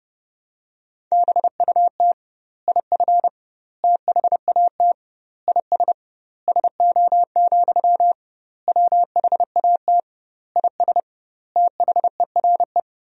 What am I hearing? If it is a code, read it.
Morse code, heard as a keyed tone, about 30 wpm, 700 Hz. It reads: BUT IF THAT IS SO, WHAT IS THERE
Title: 2x Word-Spacing - Sentences from Top 200 Words - 30wpm